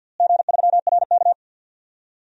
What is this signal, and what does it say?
Morse code, heard as a keyed tone, about 50 wpm, 700 Hz. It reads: G3RK
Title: DXCC-100 Call Signs - 4 Characters - Repeated 2x - 50wpm